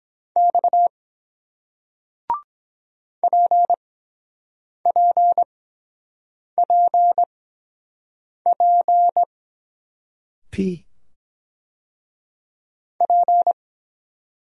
Morse code, heard as a keyed tone, about 23 wpm, spaced as for 7 wpm, 700 Hz. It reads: X PPPP P